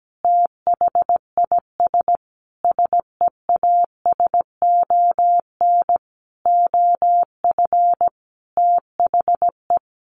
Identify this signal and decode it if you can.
Morse code, heard as a keyed tone, about 17 wpm, 700 Hz. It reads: THIS SEASON OF THE
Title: Sentences from Top 1400 Words 17wpm